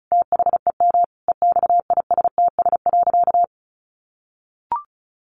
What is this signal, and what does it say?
Morse code, heard as a keyed tone, about 34 wpm, 700 Hz. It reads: THEM EXISTS.
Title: A Sound Of Thunder 35wpm